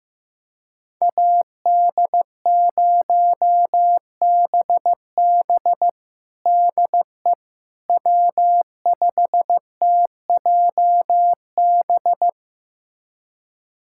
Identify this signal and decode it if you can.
Morse code, heard as a keyed tone, about 15 wpm, 700 Hz. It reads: AD0BB DE W5TJB
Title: USA Callsign DE Callsign - 5 Characters - 15wpm